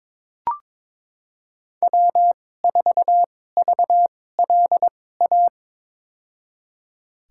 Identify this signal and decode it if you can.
Morse code, heard as a keyed tone, about 22 wpm, spaced as for 16 wpm, 700 Hz. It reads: W4VLA
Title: USA Call Signs - 5 Characters - 2x Character Spacing - 22wpm